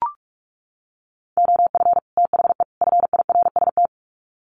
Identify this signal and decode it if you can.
Morse code, heard as a keyed tone, about 44 wpm, 700 Hz. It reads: OF THE FIRST
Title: Sets of 3 Words 45wpm